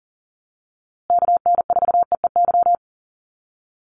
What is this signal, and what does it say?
Morse code, heard as a keyed tone, about 40 wpm, 700 Hz. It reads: KN4EEY
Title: USA Call Signs Repeated 3x 40wpm